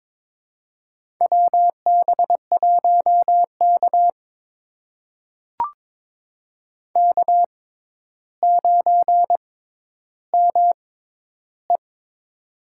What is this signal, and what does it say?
Morse code, heard as a keyed tone, about 22 wpm, spaced as for 11 wpm, 700 Hz. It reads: WB1K K9ME
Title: USA Call Signs - 6x Character Spacing - 22wpm